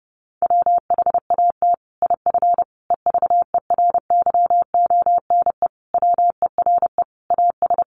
Morse code, heard as a keyed tone, about 30 wpm, 700 Hz. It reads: WHAT IF EVERYONE WERE AS
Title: Sentences from Top 1400 Words 30wpm